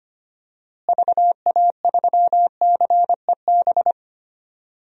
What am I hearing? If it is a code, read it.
Morse code, heard as a keyed tone, about 25 wpm, 700 Hz. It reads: VA3CEB
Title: DXCC-100 Call Signs - 6 Characters - Repeated 2x - 25wpm